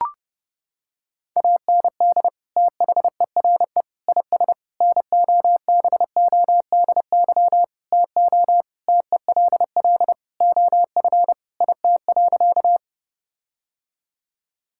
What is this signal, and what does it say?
Morse code, heard as a keyed tone, about 30 wpm, 700 Hz. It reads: AND THERE IS NOBODY TO TELL OF IT.
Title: Sentences from Top 1600 Words 30wpm